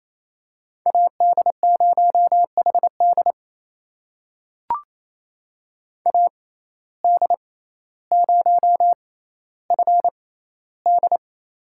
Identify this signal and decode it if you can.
Morse code, heard as a keyed tone, about 28 wpm, spaced as for 14 wpm, 700 Hz. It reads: AD0HD AD0FD
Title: USA Call Signs - 5 Characters - 6x Character Spacing - 28wpm